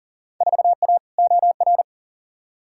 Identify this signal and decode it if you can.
Morse code, heard as a keyed tone, about 40 wpm, 700 Hz. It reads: 4A OR
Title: Field Day Exchanges - 40wpm